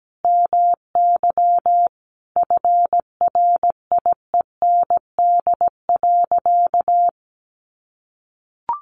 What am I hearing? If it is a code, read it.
Morse code, heard as a keyed tone, about 17 wpm, 700 Hz. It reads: MY FRIEND.
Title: Sentences from Top 1300 Words 17wpm